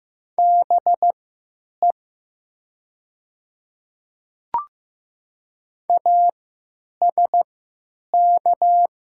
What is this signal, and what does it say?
Morse code, heard as a keyed tone, about 15 wpm, spaced as for 9 wpm, 700 Hz. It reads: BE ASK
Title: 3x Character Spacing - Top 200 Words - 15wpm